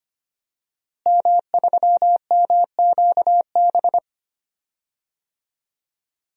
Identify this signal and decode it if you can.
Morse code, heard as a keyed tone, about 25 wpm, 700 Hz. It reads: M3MQB
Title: DXCC-100 Call Signs - 5 Characters - Repeated 3x - 25wpm